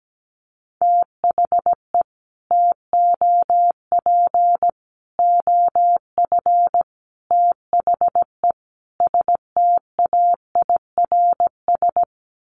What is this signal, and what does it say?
Morse code, heard as a keyed tone, about 17 wpm, 700 Hz. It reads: THE TOP OF THE STAIRS
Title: Sets of 5 Words Encore 17wpm